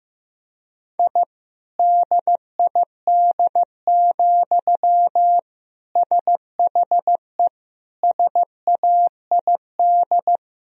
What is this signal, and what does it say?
Morse code, heard as a keyed tone, about 15 wpm, 700 Hz. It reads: I DID, SHE SAID
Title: Sentences from Top 1000 Words - Review Part 4 - 15wpm